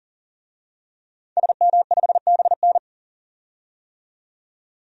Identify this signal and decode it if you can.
Morse code, heard as a keyed tone, about 40 wpm, 700 Hz. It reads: SM5BN
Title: DXCC-100 Call Signs - 5 Characters - 40wpm